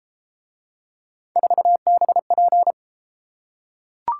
Morse code, heard as a keyed tone, about 30 wpm, 700 Hz. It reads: 4BP